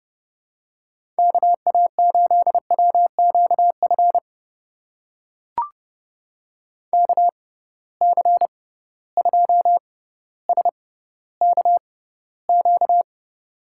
Morse code, heard as a keyed tone, about 32 wpm, spaced as for 16 wpm, 700 Hz. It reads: KA8WQF KC2SKQ
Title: USA Call Signs - 6 Characters - 6x Character Spacing - 30wpm